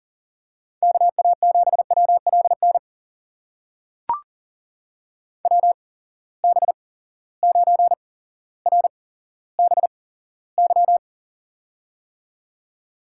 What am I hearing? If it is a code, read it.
Morse code, heard as a keyed tone, about 40 wpm, spaced as for 17 wpm, 700 Hz. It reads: KA7WLN WB9RBY
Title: USA Call Signs - 6 Characters - 8x Character Spacing - 40wpm